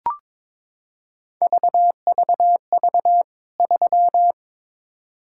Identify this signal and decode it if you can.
Morse code, heard as a keyed tone, about 22 wpm, 700 Hz. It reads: VVV 3
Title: K1USN SST QSO Practice 22wpm